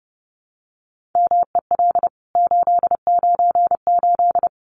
Morse code, heard as a keyed tone, about 30 wpm, 700 Hz. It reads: MEL 898
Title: CWT Exchange 30wpm